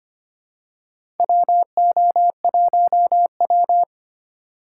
Morse code, heard as a keyed tone, about 25 wpm, 700 Hz. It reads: WO1W
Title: USA Call Signs - 4 Characters - 3x Character Spacing - 25wpm